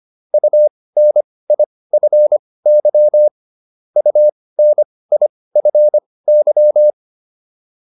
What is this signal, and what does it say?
Morse code, heard as a keyed tone, about 25 wpm, spaced as for 18 wpm, 600 Hz. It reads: UNIFY UNIFY